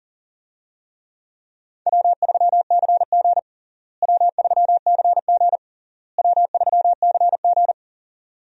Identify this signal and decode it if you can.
Morse code, heard as a keyed tone, about 37 wpm, 700 Hz. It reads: W3CG W3CG W3CG
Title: USA Call Signs - 4 Characters - Repeated 3x - 40wpm